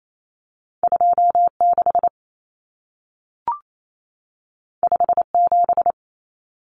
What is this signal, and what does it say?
Morse code, heard as a keyed tone, about 28 wpm, 700 Hz. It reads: 26 57
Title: Common Cut Numbers - 2-Digit 28wpm